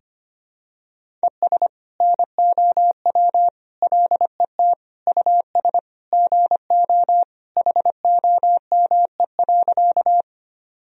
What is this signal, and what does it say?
Morse code, heard as a keyed tone, about 25 wpm, 700 Hz. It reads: ES NOW LET US GO HOME.
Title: Sentences from Top 300 Words and Abbreviations 25wpm